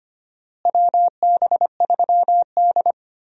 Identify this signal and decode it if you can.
Morse code, heard as a keyed tone, about 25 wpm, 700 Hz. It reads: WB3D
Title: USA Call Signs - 4 Characters - Repeated 3x - 25wpm